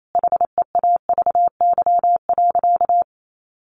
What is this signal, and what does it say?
Morse code, heard as a keyed tone, about 28 wpm, 700 Hz. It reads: HEAVY.
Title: Sentences from Top 1400 Words 28wpm